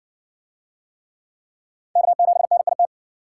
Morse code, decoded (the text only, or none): K7NIT